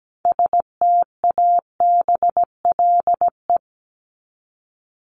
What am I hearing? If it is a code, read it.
Morse code, heard as a keyed tone, about 17 wpm, 700 Hz. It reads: STABLE
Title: Top 1251-1500 Words 17wpm